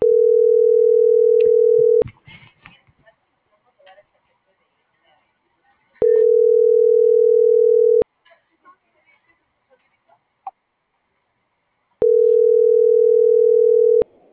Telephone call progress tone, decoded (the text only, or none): ringback tone